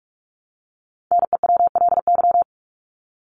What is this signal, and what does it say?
Morse code, heard as a keyed tone, about 44 wpm, 700 Hz. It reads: NEWLY
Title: LY Suffix - Top 100 Words with Base Words 45wpm